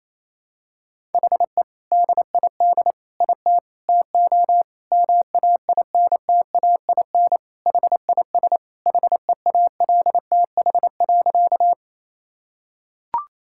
Morse code, heard as a keyed tone, about 28 wpm, 700 Hz. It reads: HE DID IT TO MAINTAIN HIS HEALTH.
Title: Sentences from Top 1000 Words - Review Part 5 - 28wpm